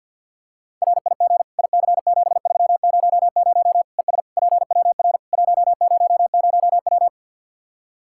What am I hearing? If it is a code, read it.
Morse code, heard as a keyed tone, about 52 wpm, 700 Hz. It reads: RIG IC7300 ES PWR 100W
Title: Practice QSO - Exchange 2 - Info - 50wpm